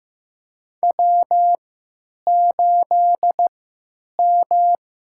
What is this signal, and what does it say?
Morse code, heard as a keyed tone, about 15 wpm, spaced as for 9 wpm, 700 Hz. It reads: W8M
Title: USA Call Signs - 3x Character Spacing - 15wpm